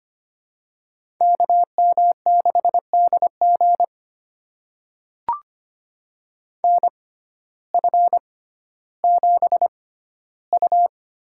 Morse code, heard as a keyed tone, about 25 wpm, spaced as for 14 wpm, 700 Hz. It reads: KM6DG NF7U
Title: USA Call Signs - 6x Character Spacing - 25wpm